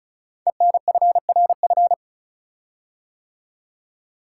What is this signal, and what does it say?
Morse code, heard as a keyed tone, about 36 wpm, 700 Hz. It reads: ENFRF
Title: Code Groups - 5 Letters - 35wpm